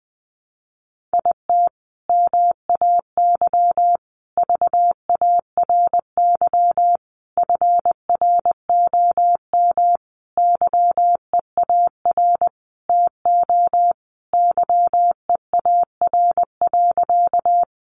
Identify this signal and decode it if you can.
Morse code, heard as a keyed tone, about 19 wpm, 700 Hz. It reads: IT MAY VARY FROM YEAR TO YEAR.
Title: Sentences from Top 1200 Words 20wpm